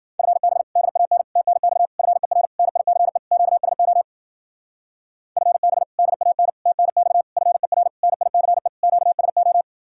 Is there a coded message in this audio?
Yes: Morse code from 0.2 to 9.6 s, heard as a keyed tone, about 52 wpm, 700 Hz. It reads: FB DAN TNX FER NICE QSO FB DAN TNX FER NICE QSO